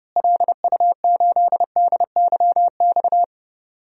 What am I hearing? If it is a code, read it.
Morse code, heard as a keyed tone, about 29 wpm, 700 Hz. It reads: LU8DYX
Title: DXCC-100 Call Signs - 6 Characters - Repeated 3x - 30wpm